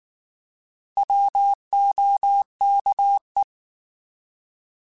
Morse code, heard as a keyed tone, about 19 wpm, 800 Hz. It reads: WOKE
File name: Alphabetical / Alphabetical mp3s / Ws @18